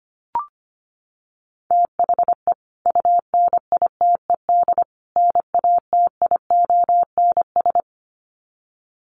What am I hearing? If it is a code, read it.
Morse code, heard as a keyed tone, about 25 wpm, 700 Hz. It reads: THE UNITED NATIONS
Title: Sets of 3 Words 25wpm